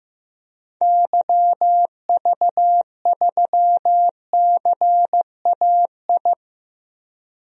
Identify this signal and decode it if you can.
Morse code, heard as a keyed tone, about 15 wpm, 700 Hz. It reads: YV3CAI
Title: DXCC-100 Call Signs - 6 Characters - Repeated 3x - 15wpm